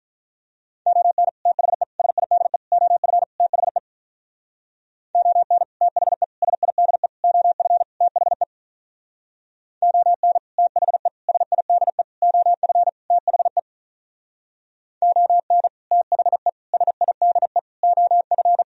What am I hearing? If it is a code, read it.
Morse code, heard as a keyed tone, about 45 wpm, 700 Hz. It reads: ON THE SIDE OF THE ON THE SIDE OF THE ON THE SIDE OF THE ON THE SIDE OF